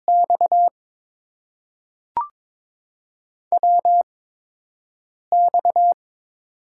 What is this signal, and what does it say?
Morse code, heard as a keyed tone, about 22 wpm, spaced as for 6 wpm, 690 Hz. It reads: X WX
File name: Top 90 QSO Elements - 8x Character Spacing - 22wpm